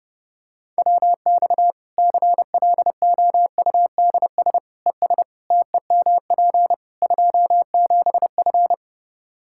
Morse code, heard as a keyed tone, about 28 wpm, 700 Hz. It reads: WX CLOUDS ES TEMP 27F
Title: Practice QSO - Exchange 2 - Info - 30wpm